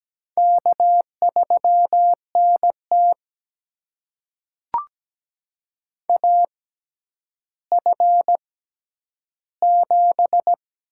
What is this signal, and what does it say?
Morse code, heard as a keyed tone, about 17 wpm, spaced as for 10 wpm, 700 Hz. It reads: K3NT AF7
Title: USA Call Signs - 4 Characters - 6x Character Spacing - 17wpm